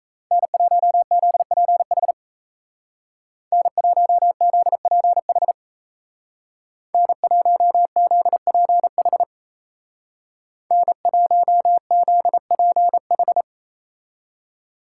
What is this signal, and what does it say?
Morse code, heard as a keyed tone, about 37 wpm, 700 Hz. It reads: N1ZPH N1ZPH N1ZPH N1ZPH